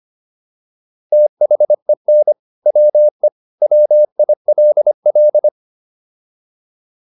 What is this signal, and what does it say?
Morse code, heard as a keyed tone, about 25 wpm, 600 Hz. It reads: THEN WE WILL